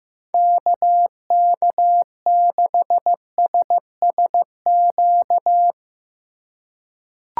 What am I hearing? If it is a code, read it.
Morse code, heard as a keyed tone, about 15 wpm, 700 Hz. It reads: KK6SSQ
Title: USA Call Signs - 6 Characters - 3x Character Spacing - 15wpm